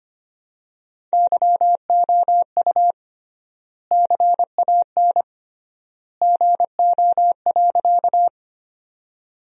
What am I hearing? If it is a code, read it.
Morse code, heard as a keyed tone, about 25 wpm, 700 Hz. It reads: YOU CAN GO.
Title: 3x Word-Spacing - Sentences from Top 100 Words - 25wpm